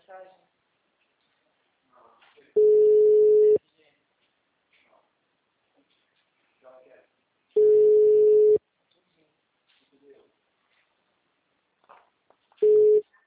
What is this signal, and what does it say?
Ringback tone (Europe, 425 Hz), with speech.
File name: out- -2051- - - .44496